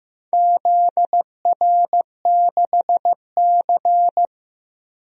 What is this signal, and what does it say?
Morse code, heard as a keyed tone, about 16 wpm, 700 Hz. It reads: ZR6C
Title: DXCC-100 Call Signs - 4 Characters - Repeated 2x - 15wpm